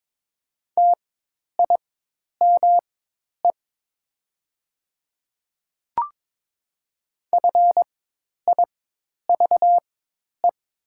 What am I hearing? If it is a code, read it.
Morse code, heard as a keyed tone, about 22 wpm, spaced as for 10 wpm, 700 Hz. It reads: TIME FIVE
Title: 4x Character Spacing - Top 500 Words - 22wpm